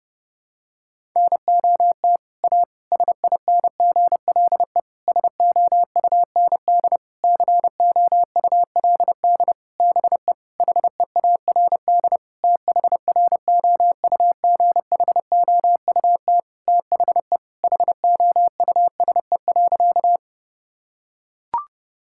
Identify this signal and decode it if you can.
Morse code, heard as a keyed tone, about 30 wpm, 700 Hz. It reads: NOT A SINGLE SOUND COULD BE HEARD THROUGHOUT THE HOUSE.
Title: Sentences from Top 1000 Words - Review Part 5 - 30wpm